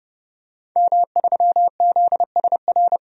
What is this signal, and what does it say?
Morse code, heard as a keyed tone, about 32 wpm, 700 Hz. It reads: M3ZSR
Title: DXCC-100 Call Signs - 5 Characters - 30wpm